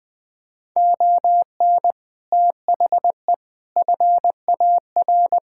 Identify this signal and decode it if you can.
Morse code, heard as a keyed tone, about 20 wpm, 700 Hz. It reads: ON THE FAR
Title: Sets of 6 Words - 20wpm